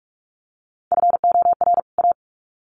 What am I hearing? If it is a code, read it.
Morse code, heard as a keyed tone, about 44 wpm, 700 Hz. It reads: FOR A
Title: Sets of 2 Words 45wpm